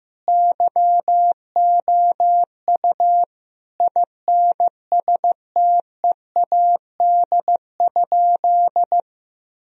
Morse code, heard as a keyed tone, about 15 wpm, 700 Hz. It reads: YOU INSTEAD?
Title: Sentences from Top 1000 Words - Review Part 4 - 15wpm